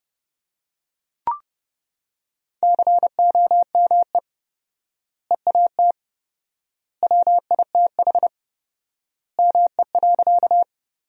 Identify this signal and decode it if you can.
Morse code, heard as a keyed tone, about 29 wpm, 700 Hz. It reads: COME EAT WITH ME.
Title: 4x Word-Spacing - Sentences from Top 300 Words - 30wpm